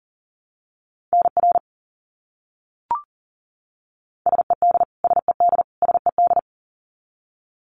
Morse code, heard as a keyed tone, about 40 wpm, 700 Hz. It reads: NR SED SED SED
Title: Abbreviations Repeated 3x - 40wpm